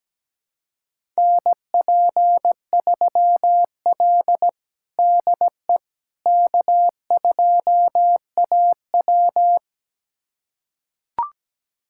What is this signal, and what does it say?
Morse code, heard as a keyed tone, about 17 wpm, 700 Hz. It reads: NP3L DE K2AW
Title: USA Callsign DE Callsign - 4 Characters - 17wpm